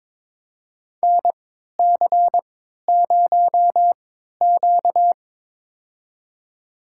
Morse code, heard as a keyed tone, about 21 wpm, spaced as for 12 wpm, 700 Hz. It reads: NC0Q
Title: USA Call Signs - 4 Characters - 3x Character Spacing - 22wpm